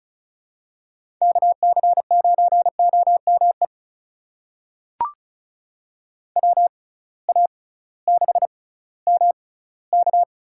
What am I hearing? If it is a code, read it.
Morse code, heard as a keyed tone, about 36 wpm, spaced as for 19 wpm, 700 Hz. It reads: KC9OME WA6MK
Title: USA Call Signs - 6 Characters - 6x Character Spacing - 35wpm